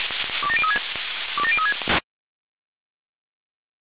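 Five-tone selective call (ZVEI), heard as quick runs of five tones, tones of about 70 ms. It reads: 27036 27036
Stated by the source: ZVEI five-tone selective call received by SDR.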